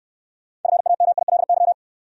Morse code, heard as a keyed tone, about 68 wpm, 690 Hz. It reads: FAMILY